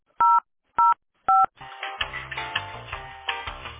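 DTMF, heard as telephone touch tones, tones of about 180 ms. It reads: 005
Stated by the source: Telephone call recording from the tesspb.mdpbx PBX.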